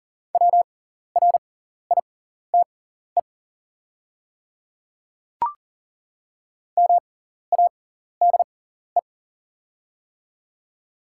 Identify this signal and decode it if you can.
Morse code, heard as a keyed tone, about 40 wpm, spaced as for 14 wpm, 700 Hz. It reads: WRITE MADE
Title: 6x Character Spacing - Top 300 Words - 40wpm